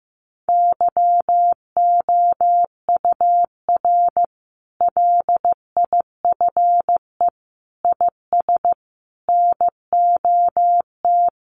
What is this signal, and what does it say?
Morse code, heard as a keyed tone, about 15 wpm, 700 Hz. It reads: YOUR LIFE IS NOT
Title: Sentences from Top 1200 Words 15wpm